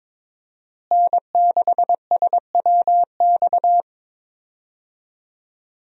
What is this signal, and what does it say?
Morse code, heard as a keyed tone, about 22 wpm, 700 Hz. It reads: N6SWX
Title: USA Call Signs - 5 Characters - Repeated 2x - 22wpm